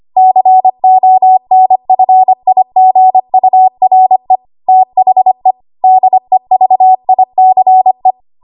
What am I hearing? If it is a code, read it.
Morse code, heard as a keyed tone, about 25 wpm, 750 Hz. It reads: CONFIGURE THE DEVICE
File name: Archive / 25 WPM / 25WPM